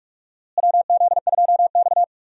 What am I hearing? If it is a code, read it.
Morse code, heard as a keyed tone, about 46 wpm, 700 Hz. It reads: WZ2X
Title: USA Call Signs - 4 Characters - Repeated 2x - 45wpm